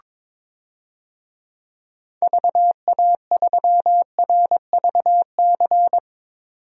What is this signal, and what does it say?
Morse code, heard as a keyed tone, about 21 wpm, 700 Hz. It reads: VA3RVC
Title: DXCC-100 Call Signs - 6 Characters - 22wpm